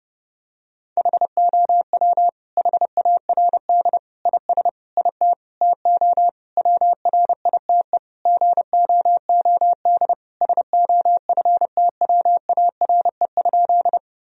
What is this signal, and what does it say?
Morse code, heard as a keyed tone, about 32 wpm, 700 Hz. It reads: HOW HARD IS IT TO WRITE GOOD SOFTWARE?
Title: Sentences from Top 1000 Words - Review Part 3 - 30wpm